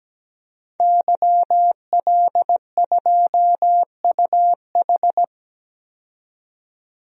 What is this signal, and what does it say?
Morse code, heard as a keyed tone, about 17 wpm, 700 Hz. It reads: YL2UH
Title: DXCC-100 Call Signs - 5 Characters - 17wpm